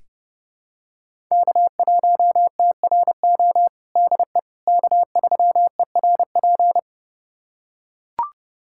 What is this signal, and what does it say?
Morse code, heard as a keyed tone, about 30 wpm, 700 Hz. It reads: K1TRO DE K3ERP
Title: USA Callsign DE Callsign 30wpm